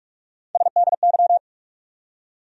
Morse code, heard as a keyed tone, about 46 wpm, 700 Hz. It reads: SDY
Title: USA Call Sign Postfixes - 3 Letters 45wpm